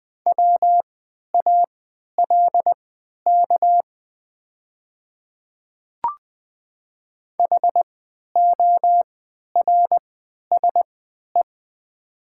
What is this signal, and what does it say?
Morse code, heard as a keyed tone, about 20 wpm, spaced as for 11 wpm, 700 Hz. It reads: WALK HORSE
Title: 3x Character Spacing - Top 500 Words - 20wpm